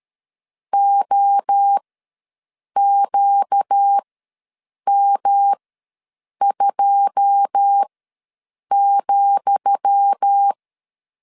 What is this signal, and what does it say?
Morse code, heard as a keyed tone, about 13 wpm, spaced as for 7 wpm, 800 Hz. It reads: OQM2,
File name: LEZIONI / 10 SIGNS / 10D COMA ws 12wpm 3618